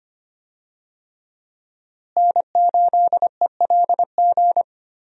Morse code, heard as a keyed tone, about 25 wpm, 700 Hz. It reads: N8ELG
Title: USA Call Signs - 5 Characters - Repeated 3x - 25wpm